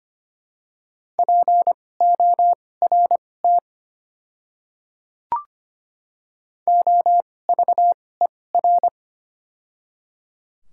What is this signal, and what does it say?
Morse code, heard as a keyed tone, about 25 wpm, spaced as for 18 wpm, 700 Hz. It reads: PORT OVER
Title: 2x Character Spacing - Top 300 Words - 25wpm